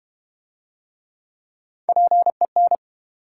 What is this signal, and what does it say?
Morse code, heard as a keyed tone, about 31 wpm, 700 Hz. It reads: PEN
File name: Easy Words - Speed-Racing with Hints - 25wpm